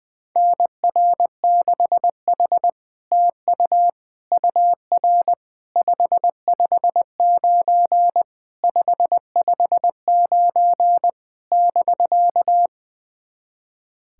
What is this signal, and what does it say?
Morse code, heard as a keyed tone, about 20 wpm, 700 Hz. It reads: NR6H TU UR 559 559 <BK>
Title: SOTA QSO Practice 20wpm